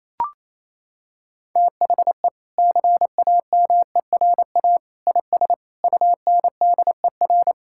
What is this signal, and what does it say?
Morse code, heard as a keyed tone, about 28 wpm, 700 Hz. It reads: THE CAMERA IS UNDER
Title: Sentences from Top 1000 Words - Review Part 5 - 28wpm